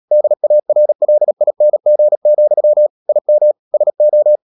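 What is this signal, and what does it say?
Morse code, heard as a keyed tone, about 37 wpm, 600 Hz. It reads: DARLING, IM SO
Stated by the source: Morse code practice at 35–39 wpm sending English text of Ken Follett's novel Never, from CW med Gnister.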